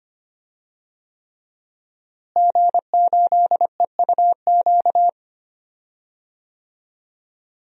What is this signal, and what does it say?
Morse code, heard as a keyed tone, about 25 wpm, 700 Hz. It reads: G8EUQ
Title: DXCC-100 Call Signs - 5 Characters - Repeated 3x - 25wpm